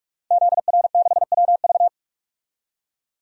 Morse code, heard as a keyed tone, about 44 wpm, 700 Hz. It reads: ZR6WV
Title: DXCC-100 Call Signs - 5 Characters - 45wpm